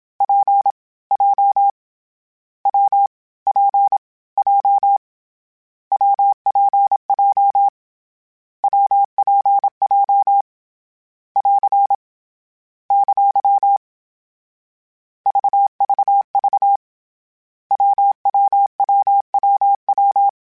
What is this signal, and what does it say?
Morse code, heard as a keyed tone, about 27 wpm, 800 Hz. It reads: P J W P J WPJ WPJ <AR> ! VVV WWWWW
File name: Variable Speed / Variable Speed-mp3s / WPJ WPJ1 sp @25